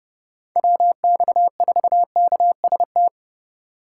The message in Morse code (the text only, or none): WX4KST